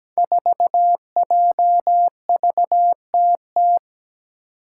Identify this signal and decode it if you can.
Morse code, heard as a keyed tone, about 17 wpm, 700 Hz. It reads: 4JVTT
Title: Five Challenging Letter-Number Combinations 17wpm